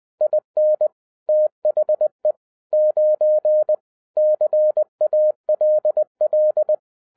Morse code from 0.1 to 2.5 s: IN THE